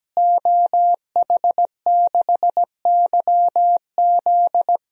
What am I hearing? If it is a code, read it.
Morse code, heard as a keyed tone, about 17 wpm, 700 Hz. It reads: OH6YZ
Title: DXCC-100 Call Signs - 5 Characters - Repeated 2x - 17wpm